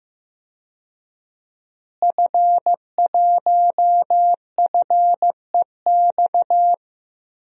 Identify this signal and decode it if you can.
Morse code, heard as a keyed tone, about 15 wpm, 700 Hz. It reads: F1FEX
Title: DXCC-100 Call Signs - 5 Characters - Repeated 2x - 15wpm